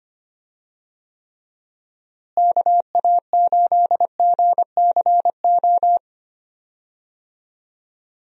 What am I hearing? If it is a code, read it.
Morse code, heard as a keyed tone, about 25 wpm, 700 Hz. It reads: KA8GCO
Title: USA Call Signs - 6 Characters - Repeated 2x - 25wpm